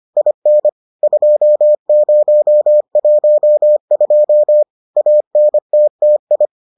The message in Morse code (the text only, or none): IN 2012 ANTTI